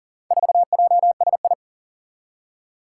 Morse code, heard as a keyed tone, about 40 wpm, 700 Hz. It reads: 4JSI
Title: Four Challenging Letter-Number Combinations 40wpm